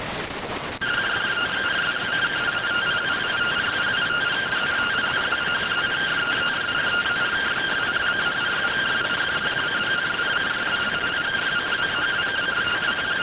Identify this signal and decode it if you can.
Two-tone teleprinter signal, about 170 Hz shift.